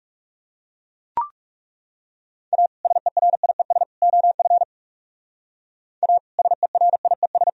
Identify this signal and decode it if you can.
Morse code, heard as a keyed tone, about 38 wpm, 700 Hz. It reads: A SERIES OF A SERIES